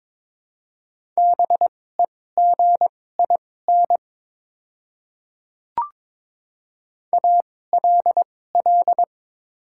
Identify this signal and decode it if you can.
Morse code, heard as a keyed tone, about 22 wpm, spaced as for 16 wpm, 700 Hz. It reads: BEGIN ALL
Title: 2x Character Spacing - Top 500 Words - 22wpm